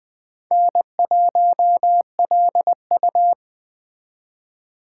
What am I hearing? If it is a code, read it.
Morse code, heard as a keyed tone, about 19 wpm, 700 Hz. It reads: N1LU